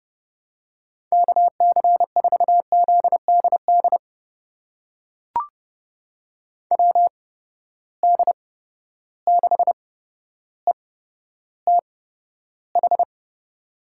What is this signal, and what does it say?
Morse code, heard as a keyed tone, about 32 wpm, spaced as for 13 wpm, 700 Hz. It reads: KC4ZDD WD6ETH
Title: USA Call Signs - 6 Characters - 8x Character Spacing - 30wpm